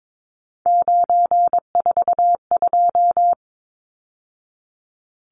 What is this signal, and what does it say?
Morse code, heard as a keyed tone, about 22 wpm, 700 Hz. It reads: 942
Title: Numbers - 3-Digit 22wpm